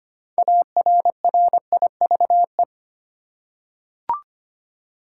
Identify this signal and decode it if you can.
Morse code, heard as a keyed tone, about 25 wpm, 700 Hz. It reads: ARRIVE
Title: Six Letter Words - 25wpm